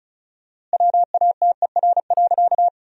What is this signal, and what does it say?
Morse code, heard as a keyed tone, about 34 wpm, 700 Hz. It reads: WATER.
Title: 4x Word-Spacing - Sentences from Top 100 Words - 35wpm